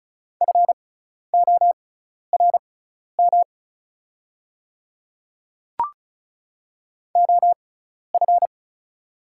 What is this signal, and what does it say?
Morse code, heard as a keyed tone, about 36 wpm, spaced as for 12 wpm, 700 Hz. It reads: FORM OF
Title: 6x Character Spacing - Top 300 Words - 35wpm